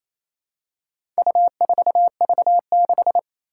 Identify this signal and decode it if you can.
Morse code, heard as a keyed tone, about 28 wpm, 700 Hz. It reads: U4V6
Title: Four Challenging Letter-Number Combinations 28wpm